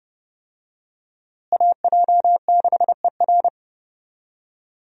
Morse code, heard as a keyed tone, about 32 wpm, 700 Hz. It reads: AJ6ER
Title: USA Call Signs - 5 Characters - Repeated 3x - 30wpm